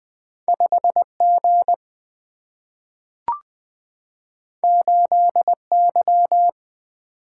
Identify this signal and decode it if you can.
Morse code, heard as a keyed tone, about 20 wpm, 700 Hz. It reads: 5 G 8 Y
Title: Single-Letter Single-Number 20wpm